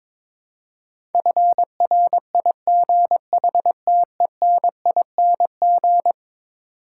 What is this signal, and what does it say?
Morse code, heard as a keyed tone, about 22 wpm, 700 Hz. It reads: FRIGHTENING
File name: practice / 22wpm / Eleven Letter Words - 22wpm